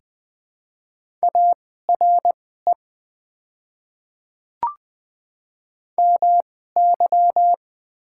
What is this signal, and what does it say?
Morse code, heard as a keyed tone, about 20 wpm, spaced as for 15 wpm, 700 Hz. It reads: ARE MY
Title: 2x Character Spacing - Top 100 Words - 20wpm